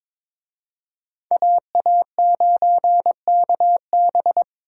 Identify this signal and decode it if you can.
Morse code, heard as a keyed tone, about 26 wpm, 700 Hz. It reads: AA9KB
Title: Speed Racing - USA Call Signs 22wpm